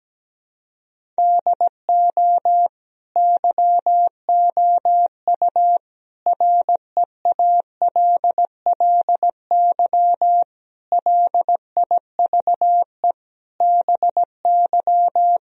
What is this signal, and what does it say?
Morse code, heard as a keyed tone, about 17 wpm, 700 Hz. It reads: DO YOU REALLY LIVE BY
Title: Sentences from Top 1000 Words - Review Part 1 - 17wpm